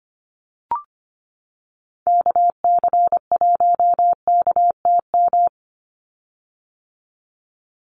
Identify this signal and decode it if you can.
Morse code, heard as a keyed tone, about 25 wpm, 700 Hz. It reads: KC1KTM
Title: USA Call Signs 25wpm